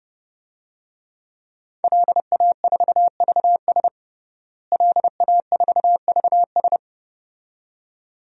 Morse code, heard as a keyed tone, about 32 wpm, 700 Hz. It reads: LA4VS LA4VS
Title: DXCC-100 Call Signs - 5 Characters - Repeated 2x - 30wpm